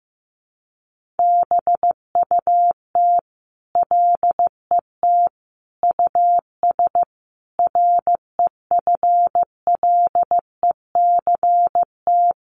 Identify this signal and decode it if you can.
Morse code, heard as a keyed tone, about 15 wpm, 700 Hz. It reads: BUT LET US REFLECT
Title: Sentences from Top 1500 Words 15wpm